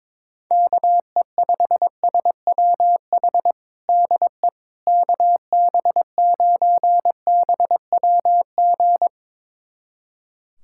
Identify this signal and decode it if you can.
Morse code, heard as a keyed tone, about 22 wpm, 700 Hz. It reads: KE5SWH DE KB9BWG
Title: USA Callsign DE Callsign - 6 Characters - 22wpm